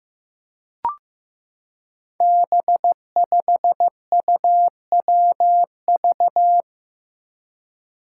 Morse code, heard as a keyed tone, about 15 wpm, 700 Hz. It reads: B5UWV
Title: Five Challenging Letter-Number Combinations 15wpm